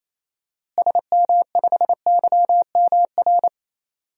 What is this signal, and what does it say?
Morse code, heard as a keyed tone, about 28 wpm, 700 Hz. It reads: SM5YMR
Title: DXCC-100 Call Signs - 6 Characters - Repeated 3x - 28wpm